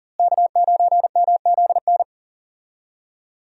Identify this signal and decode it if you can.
Morse code, heard as a keyed tone, about 40 wpm, 700 Hz. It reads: K9MZN
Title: USA Call Signs - 5 Characters - Repeated 3x - 40wpm